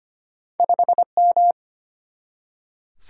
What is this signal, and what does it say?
Morse code, heard as a keyed tone, about 25 wpm, 700 Hz. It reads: 5M
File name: Two Letter-Number Combinations - 25wpm